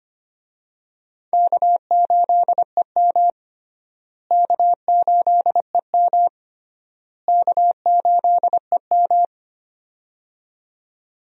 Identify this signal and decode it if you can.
Morse code, heard as a keyed tone, about 25 wpm, 700 Hz. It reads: K8EM K8EM K8EM
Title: USA Call Signs - 4 Characters - Repeated 3x - 25wpm